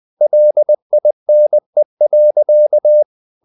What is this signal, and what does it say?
Morse code, heard as a keyed tone, about 20 wpm, 600 Hz. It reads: LINE.
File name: the old man and the sea part 2 20wpm